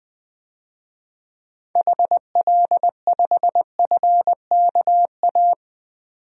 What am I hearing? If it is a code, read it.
Morse code, heard as a keyed tone, about 21 wpm, 700 Hz. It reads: HL5FKA
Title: DXCC-100 Call Signs - 6 Characters - 20wpm